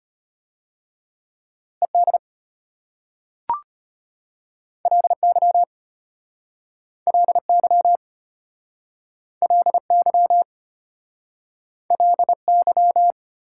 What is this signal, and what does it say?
Morse code, heard as a keyed tone, about 34 wpm, 700 Hz. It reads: ED LY LY LY LY